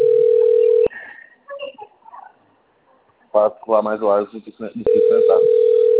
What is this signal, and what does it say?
Ringback tone (North America), 440 + 480 Hz, with speech.
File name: out- - - - - .93411